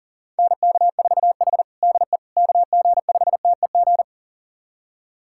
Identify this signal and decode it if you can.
Morse code, heard as a keyed tone, about 37 wpm, 700 Hz. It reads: NK4H DE KG5TEG
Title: USA Callsign DE Callsign 40wpm